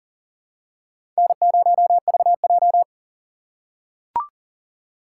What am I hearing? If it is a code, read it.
Morse code, heard as a keyed tone, about 37 wpm, 700 Hz. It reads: N0VJ
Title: USA Call Signs - 6x Character Spacing - 40wpm